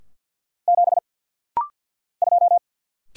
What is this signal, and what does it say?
Morse code, heard as a keyed tone, about 49 wpm, 700 Hz. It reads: 7 2
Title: Single Letter-Number - Rapid-Fire 50wpm